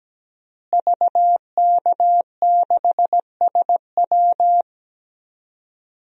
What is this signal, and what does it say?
Morse code, heard as a keyed tone, about 17 wpm, 700 Hz. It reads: VK6SW
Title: DXCC-100 Call Signs - 5 Characters - Repeated 2x - 17wpm